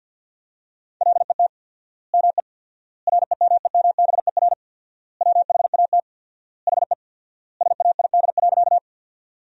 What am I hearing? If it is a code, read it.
Morse code, heard as a keyed tone, about 52 wpm, 700 Hz. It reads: LET ME REMEMBER WHAT HE SAID.
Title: 4x Word-Spacing - Sentences from Top 400 Words - 50wpm